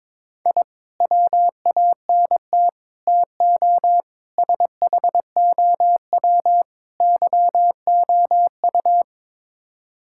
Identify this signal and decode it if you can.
Morse code, heard as a keyed tone, about 21 wpm, 700 Hz. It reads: I WANT TO SHOW YOU
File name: practice / 22wpm / Build A Phrase - 5-Word Phrases - 22wpm